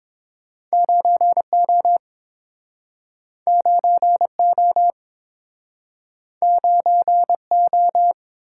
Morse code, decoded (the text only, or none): 9O 9O 9O